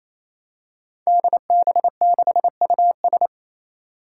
Morse code, heard as a keyed tone, about 28 wpm, 700 Hz. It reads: DB6US
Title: Five Challenging Letter-Number Combinations 28wpm